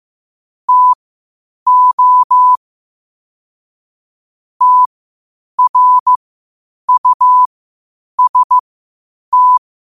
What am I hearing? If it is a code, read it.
Morse code, heard as a keyed tone, about 15 wpm, spaced as for 8 wpm, 1000 Hz. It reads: TO TRUST
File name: qotdfiles / 8wpm